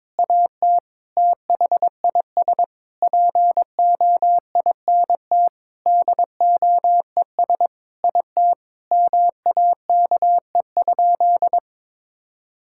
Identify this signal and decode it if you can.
Morse code, heard as a keyed tone, about 22 wpm, 700 Hz. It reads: AT THIS POINT DOES IT MAKE?
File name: practice / 22wpm / Sentences with 8-Word Phrases - 22wpm